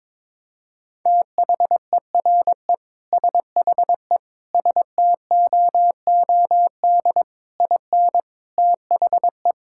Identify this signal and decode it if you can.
Morse code, heard as a keyed tone, about 22 wpm, 700 Hz. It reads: THERE SHE STOOD IN THE
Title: Sentences from Top 1600 Words 22wpm